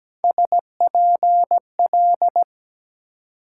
Morse code, heard as a keyed tone, about 21 wpm, 700 Hz. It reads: SPL
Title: Speed-Racing Multi-letter Phonemes - 17wpm